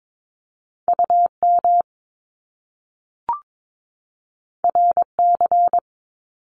Morse code, heard as a keyed tone, about 22 wpm, 700 Hz. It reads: UM RC
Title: Top Two Letter Combinations in Words 22wpm